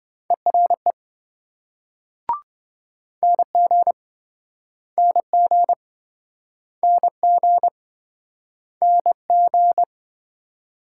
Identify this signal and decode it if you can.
Morse code, heard as a keyed tone, about 25 wpm, 700 Hz. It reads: ERE NG NG NG NG